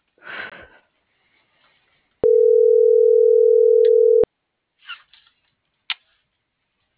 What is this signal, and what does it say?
Ringback tone (North America), 440 + 480 Hz.